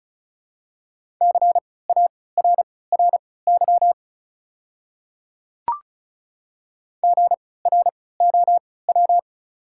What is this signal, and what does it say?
Morse code, heard as a keyed tone, about 34 wpm, spaced as for 20 wpm, 700 Hz. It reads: CARRY GROW
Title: 3x Character Spacing - Top 300 Words - 35wpm